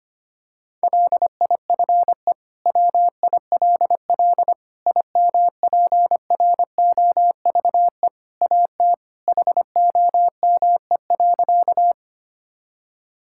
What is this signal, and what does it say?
Morse code, heard as a keyed tone, about 25 wpm, 700 Hz. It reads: LIFE WILL IMPROVE AT HOME.
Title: Sentences from Top 1000 Words - Review Part 5 - 25wpm